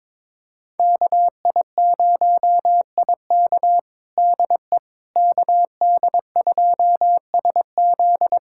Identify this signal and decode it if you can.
Morse code, heard as a keyed tone, about 21 wpm, 700 Hz. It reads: KI0IK DE KD2SZ
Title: USA Callsign DE Callsign - 5 Characters - 22wpm